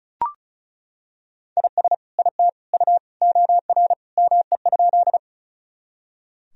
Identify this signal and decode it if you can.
Morse code, heard as a keyed tone, about 34 wpm, 700 Hz. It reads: IS IT U OR ME?
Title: Sentences from Top 200 Words and Abbreviations 35wpm